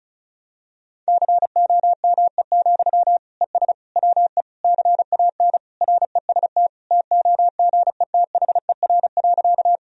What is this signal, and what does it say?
Morse code, heard as a keyed tone, about 34 wpm, 700 Hz. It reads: COME, ES WE CAN REST TOGETHER.
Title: Sentences from Top 500 Words and Abbreviations 35wpm